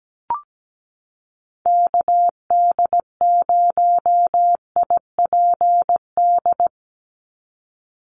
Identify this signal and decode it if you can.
Morse code, heard as a keyed tone, about 17 wpm, 700 Hz. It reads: KD0IPD
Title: USA Call Signs 17wpm